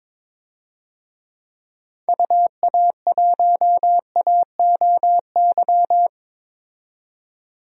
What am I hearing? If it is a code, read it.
Morse code, heard as a keyed tone, about 22 wpm, 700 Hz. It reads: UA1AOY
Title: DXCC-100 Call Signs - 6 Characters - 22wpm